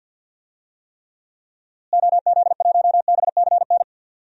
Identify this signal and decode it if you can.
Morse code, heard as a keyed tone, about 47 wpm, 700 Hz. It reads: OZ1BCN
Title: DXCC-100 Call Signs - 6 Characters - Repeated 3x - 50wpm